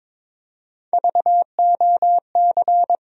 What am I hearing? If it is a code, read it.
Morse code, heard as a keyed tone, about 22 wpm, 700 Hz. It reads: VOC
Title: USA Call Sign Postfixes - 3 Letters 22wpm